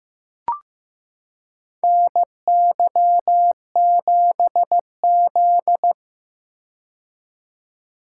Morse code, heard as a keyed tone, about 15 wpm, 700 Hz. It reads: NY7Z
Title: K1USN SST QSO Practice - 15wpm